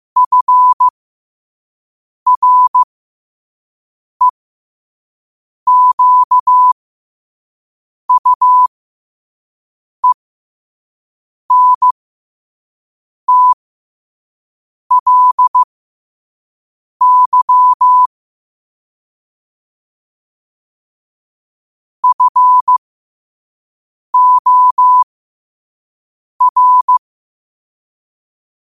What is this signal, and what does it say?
Morse code, heard as a keyed tone, about 15 wpm, spaced as for 5 wpm, 1000 Hz. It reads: FREQUENTLY FOR
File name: qotdfiles / 5wpm